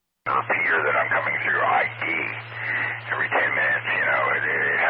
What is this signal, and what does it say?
Emergency Alert System (SAME) tones, heard as harsh buzzing bursts, with speech. It is heard from 1.0 to 4.5 s.